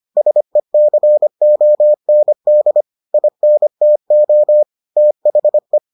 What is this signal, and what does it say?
Morse code, heard as a keyed tone, about 25 wpm, 600 Hz. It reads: SECOND INTO THE